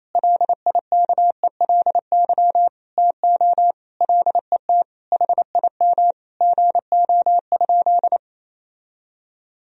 Morse code, heard as a keyed tone, about 28 wpm, 700 Hz. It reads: LIKELY TO LET HIM GO?
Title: Sentences from Top 1000 Words - Review Part 4 - 28wpm